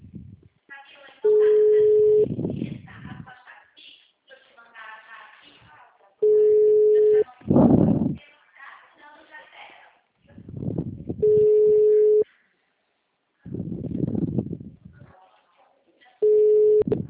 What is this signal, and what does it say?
Ringback tone (Europe, 425 Hz), with speech.
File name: out- -2009- - - .269